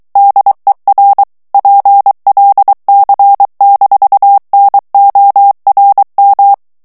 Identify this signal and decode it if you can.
Morse code, heard as a keyed tone, about 23 wpm, 800 Hz. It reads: DER PLC-NORM
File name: ALLEMAND texte clair 1 vitesse 22 mots minute